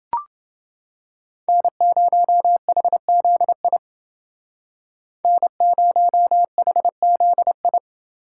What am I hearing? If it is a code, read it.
Morse code, heard as a keyed tone, about 28 wpm, 700 Hz. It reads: N0HZI N0HZI